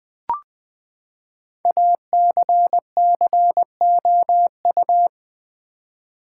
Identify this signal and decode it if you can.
Morse code, heard as a keyed tone, about 20 wpm, 700 Hz. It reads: ACCOU
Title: Build A Word - 5 Letters At A Time - 10-Letter Words - 20wpm